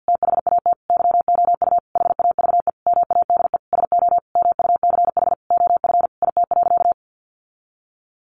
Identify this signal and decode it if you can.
Morse code, heard as a keyed tone, about 50 wpm, 700 Hz. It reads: THAT YOU HAVE MADE SO MUCH OF IT.
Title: Sentences from Top 1400 Words 50wpm